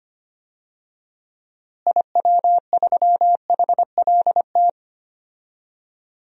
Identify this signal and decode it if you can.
Morse code, heard as a keyed tone, about 25 wpm, 700 Hz. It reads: IW3HLT
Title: DXCC-100 Call Signs - 6 Characters - Repeated 3x - 25wpm